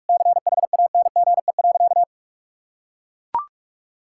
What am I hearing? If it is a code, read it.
Morse code, heard as a keyed tone, about 41 wpm, 700 Hz. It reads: CHANGE.
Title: Sentences from Top 1600 Words 45wpm